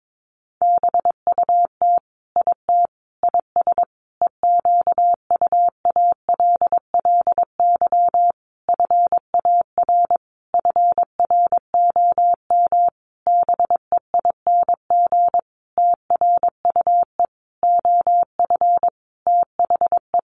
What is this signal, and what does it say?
Morse code, heard as a keyed tone, about 22 wpm, 700 Hz. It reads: BUT IT IS EQUALLY FAR FROM BEING TRUE OF THE
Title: Sentences from Top 1400 Words 22wpm